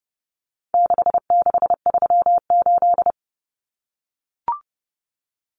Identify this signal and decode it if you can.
Morse code, heard as a keyed tone, about 30 wpm, 700 Hz. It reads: 6638
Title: Numbers - 4-Digit 30wpm